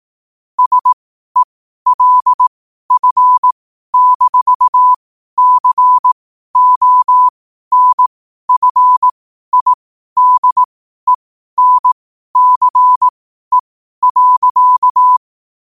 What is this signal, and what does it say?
Morse code, heard as a keyed tone, about 18 wpm, spaced as for 12 wpm, 1000 Hz. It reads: SELF-CONFIDENCE.
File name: qotdfiles / 12wpm